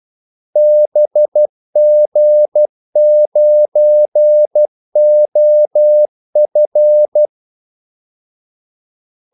Morse code, heard as a keyed tone, about 12 wpm, 600 Hz. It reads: BG9OF
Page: 2 ekstra ordmellemrum mellem kaldesignalerne